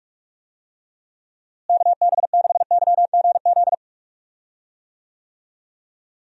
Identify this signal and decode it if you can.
Morse code, heard as a keyed tone, about 46 wpm, 700 Hz. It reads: KB6YGZ
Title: USA Call Signs - 6 Characters - Repeated 2x - 45wpm